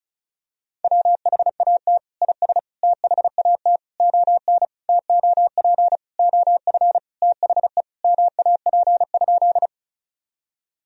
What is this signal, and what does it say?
Morse code, heard as a keyed tone, about 36 wpm, 700 Hz. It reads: WHAT IS THAT ON TOP OF THE MAP?
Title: Sentences from Top 1000 Words - Review Part 2 - 35wpm